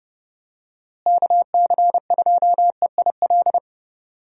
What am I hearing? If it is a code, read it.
Morse code, heard as a keyed tone, about 30 wpm, 700 Hz. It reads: KC2EIL
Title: USA Call Signs - 6 Characters - Repeated 2x - 30wpm